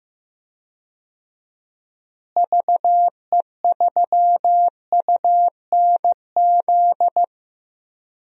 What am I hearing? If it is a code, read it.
Morse code, heard as a keyed tone, about 15 wpm, 700 Hz. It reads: VE3UNZ
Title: DXCC-100 Call Signs - 6 Characters - Repeated 2x - 15wpm